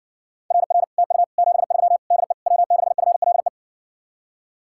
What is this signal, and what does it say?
Morse code, heard as a keyed tone, about 70 wpm, 700 Hz. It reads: RR TU 73 DE W6FLE